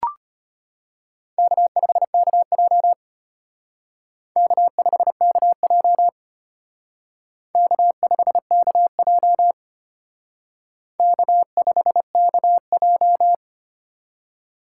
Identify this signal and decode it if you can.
Morse code, heard as a keyed tone, about 32 wpm, 700 Hz. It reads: K5KJ K5KJ K5KJ K5KJ